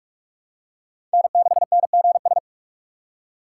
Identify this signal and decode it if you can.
Morse code, heard as a keyed tone, about 42 wpm, 700 Hz. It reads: N6NGS
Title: USA Call Signs - 5 Characters - 4x Character Spacing - 45wpm